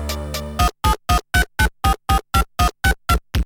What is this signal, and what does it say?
There is no code.